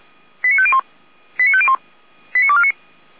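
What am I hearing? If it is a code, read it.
Five-tone selective call (ZVEI), heard as quick runs of five tones, tones of about 70 ms. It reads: 89501 89501 89249
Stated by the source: Received on SDR, ZVEI five-tone selective call 89123 for the Orlamünde fire brigade (feuerwehr-orlamuende).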